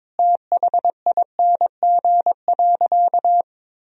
Morse code, heard as a keyed tone, about 22 wpm, 700 Hz. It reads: THING.
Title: Sentences from Top 1000 Words - Review Part 1 - 22wpm